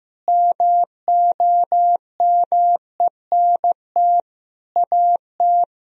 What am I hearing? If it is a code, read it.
Morse code, heard as a keyed tone, about 15 wpm, 700 Hz. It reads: MOMENT AT
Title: Sentences from Top 1000 Words - Review Part 2 - 15wpm